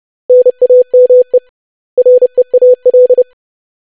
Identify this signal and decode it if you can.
Morse code, heard as a keyed tone, about 30 wpm, 500 Hz. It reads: NAME REAL
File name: lesson / web / basic500-25wpm